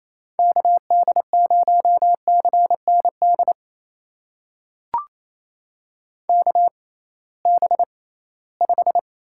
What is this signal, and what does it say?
Morse code, heard as a keyed tone, about 27 wpm, spaced as for 18 wpm, 700 Hz. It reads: KD0CND KB5
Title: USA Call Signs - 6 Characters - 6x Character Spacing - 28wpm